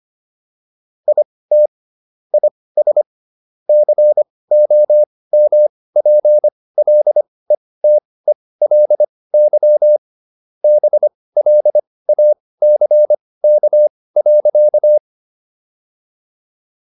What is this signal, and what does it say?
Morse code, heard as a keyed tone, about 25 wpm, spaced as for 18 wpm, 600 Hz. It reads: IT IS COMPLETELY BLACK.